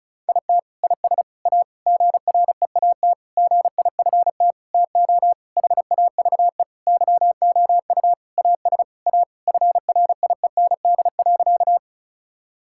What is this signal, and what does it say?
Morse code, heard as a keyed tone, about 36 wpm, 700 Hz. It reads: IT IS A GREAT GIFT TO HAVE YOU AS A FRIEND.
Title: Sentences from Top 1000 Words - Review Part 6 - 35wpm